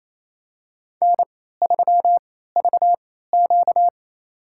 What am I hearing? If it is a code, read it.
Morse code, heard as a keyed tone, about 27 wpm, spaced as for 16 wpm, 700 Hz. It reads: N3VQ
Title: USA Call Signs - 4 Characters - 3x Character Spacing - 28wpm